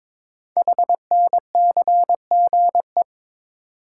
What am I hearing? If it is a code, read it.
Morse code, heard as a keyed tone, about 22 wpm, 700 Hz. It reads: HNCGE
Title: Code Groups - 5 Letters - 22wpm